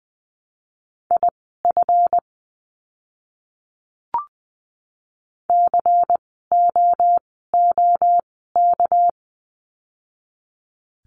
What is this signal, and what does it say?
Morse code, heard as a keyed tone, about 20 wpm, spaced as for 14 wpm, 700 Hz. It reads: IF COOK
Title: 199 Lesson 32f - Letter k - Words 20wpm